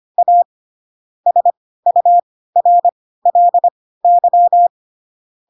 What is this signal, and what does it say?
Morse code, heard as a keyed tone, about 25 wpm, spaced as for 16 wpm, 700 Hz. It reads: A SURLY
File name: Pinocchio / Pinocchio ch2 10 25 ik1hge